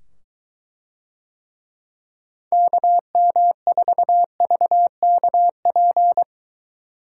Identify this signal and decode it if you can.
Morse code, heard as a keyed tone, about 22 wpm, 700 Hz. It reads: KM4VKP